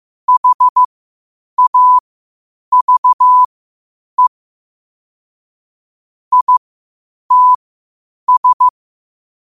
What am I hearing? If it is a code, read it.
Morse code, heard as a keyed tone, about 15 wpm, spaced as for 8 wpm, 1000 Hz. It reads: HAVE ITS
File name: qotdfiles / 8wpm